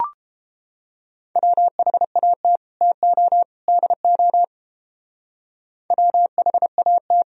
Morse code, heard as a keyed tone, about 30 wpm, 700 Hz. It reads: WHAT TO DO WHAT